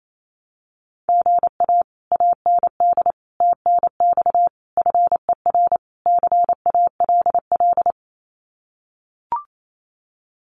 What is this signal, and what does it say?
Morse code, heard as a keyed tone, about 28 wpm, 700 Hz. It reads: GA AND TNX FER CALL
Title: Common QSO Phrases Encore 28wpm